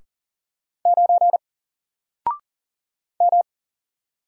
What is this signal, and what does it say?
Morse code, heard as a keyed tone, about 40 wpm, 700 Hz. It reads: 9 M
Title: Single Letter-Number Alt 40wpm